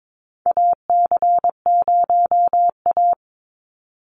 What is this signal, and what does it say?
Morse code, heard as a keyed tone, about 22 wpm, 700 Hz. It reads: AC0A
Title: NA SOTA Chaser Callsigns 22wpm